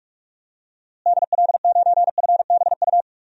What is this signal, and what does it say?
Morse code, heard as a keyed tone, about 46 wpm, 700 Hz. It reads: DL9FBU
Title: DXCC-100 Call Signs - 6 Characters - Repeated 3x - 45wpm